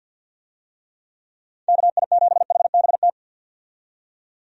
Morse code, heard as a keyed tone, about 52 wpm, 700 Hz. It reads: KI7HBT
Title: USA Call Signs - 6 Characters - 50wpm